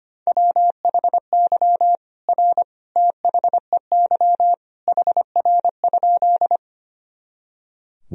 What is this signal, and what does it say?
Morse code, heard as a keyed tone, about 25 wpm, 700 Hz. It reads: WHY R THEY HR?
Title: Sentences from Top 200 Words and Abbreviations 25wpm